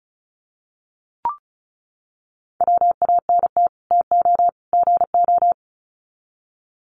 Morse code, heard as a keyed tone, about 34 wpm, 700 Hz. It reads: WANT TO GO
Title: Sets of 3 Words 35wpm